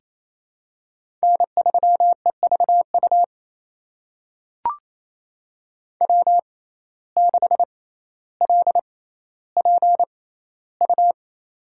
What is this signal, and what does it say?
Morse code, heard as a keyed tone, about 27 wpm, spaced as for 14 wpm, 700 Hz. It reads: N3EVU W6LPU
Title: USA Call Signs - 5 Characters - 6x Character Spacing - 28wpm